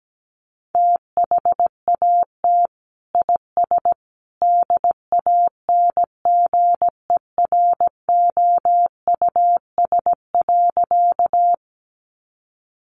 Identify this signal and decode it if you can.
Morse code, heard as a keyed tone, about 17 wpm, 700 Hz. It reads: THAT IS DANGEROUS.
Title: Sentences from Top 1200 Words 17wpm